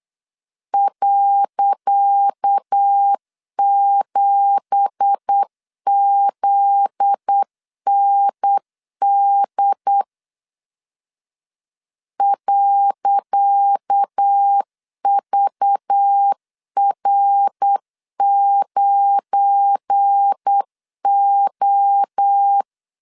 Morse code, heard as a keyed tone, about 8 wpm, 800 Hz. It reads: .7ZND .VR9O
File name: LEZIONI / 10 SIGNS / 10C DOT 8wpm 3649